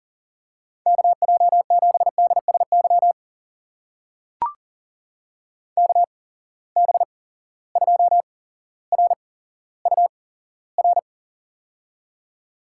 Morse code, heard as a keyed tone, about 40 wpm, spaced as for 17 wpm, 700 Hz. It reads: KJ7DSY KB2RUR
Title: USA Call Signs - 6 Characters - 8x Character Spacing - 40wpm